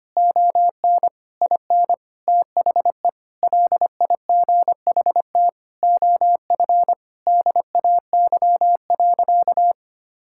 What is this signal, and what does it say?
Morse code, heard as a keyed tone, about 25 wpm, 700 Hz. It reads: ON IN THE LIGHT OF DAY.
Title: Sentences from Top 1000 Words - Review Part 5 - 25wpm